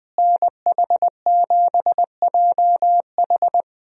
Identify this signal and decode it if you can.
Morse code, heard as a keyed tone, about 20 wpm, 700 Hz. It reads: NH7JH
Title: USA Call Signs - 5 Characters - Repeated 2x - 20wpm